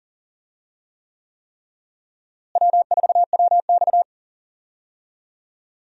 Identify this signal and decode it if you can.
Morse code, heard as a keyed tone, about 40 wpm, 700 Hz. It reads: W4WX
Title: USA Call Signs - 4 Characters - Repeated 3x - 40wpm